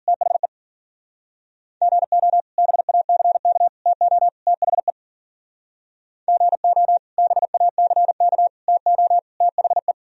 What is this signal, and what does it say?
Morse code, heard as a keyed tone, about 49 wpm, 700 Hz. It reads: THE GO BACK TO THE GO BACK TO THE
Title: Speed-Racing - Sets of 4 Words 40wpm